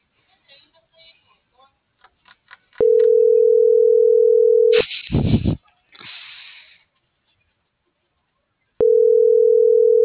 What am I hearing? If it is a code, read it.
Ringback tone (North America), 440 + 480 Hz.